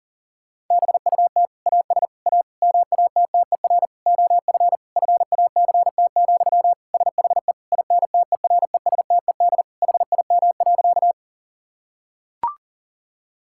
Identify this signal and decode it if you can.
Morse code, heard as a keyed tone, about 37 wpm, 700 Hz. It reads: BUT AS A MATTER OF FACT, SHE INTERESTED HIM.
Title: Sentences from Top 1000 Words - Review Part 6 - 40wpm